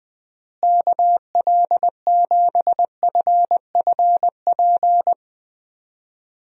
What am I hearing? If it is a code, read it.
Morse code, heard as a keyed tone, about 21 wpm, 700 Hz. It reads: KL7FFP
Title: USA Call Signs - 6 Characters - Repeated 3x - 20wpm